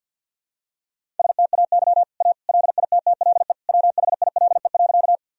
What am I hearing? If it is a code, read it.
Morse code, heard as a keyed tone, about 52 wpm, 700 Hz. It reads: STAY A LITTLE WHILE.
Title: Sentences from Top 1000 Words - Review Part 1 - 50wpm